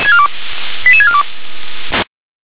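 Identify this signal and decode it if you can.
Five-tone selective call (ZVEI), heard as quick runs of five tones, tones of about 80 ms. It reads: 88642
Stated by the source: ZVEI five-tone call received on SDR.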